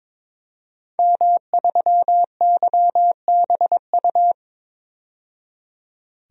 Morse code, heard as a keyed tone, about 22 wpm, 700 Hz. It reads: M3YBU
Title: DXCC-100 Call Signs - 5 Characters - 22wpm